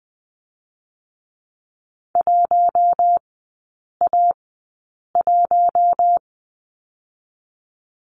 Morse code, heard as a keyed tone, about 20 wpm, spaced as for 8 wpm, 700 Hz. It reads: 1A1
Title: 041 Lesson 8e - Number 1 - Three Characters 20wpm